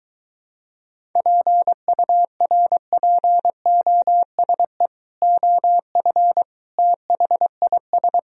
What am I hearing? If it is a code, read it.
Morse code, heard as a keyed tone, about 23 wpm, 700 Hz. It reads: PURPOSE OF THIS